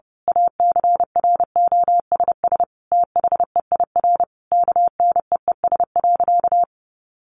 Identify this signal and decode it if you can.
Morse code, heard as a keyed tone, about 30 wpm, 700 Hz. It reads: ACROSS THEIR KNEES.
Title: A Sound Of Thunder 30wpm